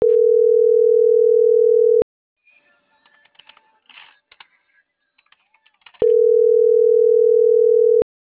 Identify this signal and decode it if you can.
Ringback tone (North America), 440 + 480 Hz.